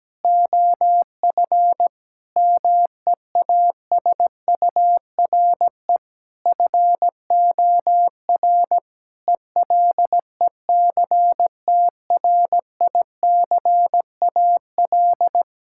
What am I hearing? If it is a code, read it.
Morse code, heard as a keyed tone, about 17 wpm, 700 Hz. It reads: OF MEASURE FOR ELECTRICAL
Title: Sentences from Top 1000 Words - Review Part 6 - 17wpm